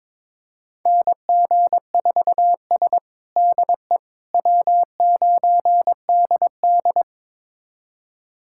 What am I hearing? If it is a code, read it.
Morse code, heard as a keyed tone, about 22 wpm, 700 Hz. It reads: NG4S DE W9DD
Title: USA Callsign DE Callsign - 4 Characters - 22wpm